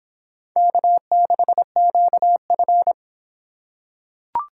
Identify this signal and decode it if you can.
Morse code, heard as a keyed tone, about 25 wpm, 700 Hz. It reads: K6QF